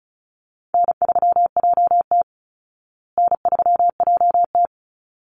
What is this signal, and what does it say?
Morse code, heard as a keyed tone, about 34 wpm, 700 Hz. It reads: N3JT N3JT
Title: Easy Call Signs Repeated 2x - 35wpm